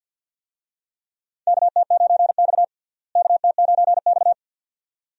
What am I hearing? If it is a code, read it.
Morse code, heard as a keyed tone, about 51 wpm, 700 Hz. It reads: KT9X KT9X
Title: USA Call Signs - 4 Characters - Repeated 2x - 50wpm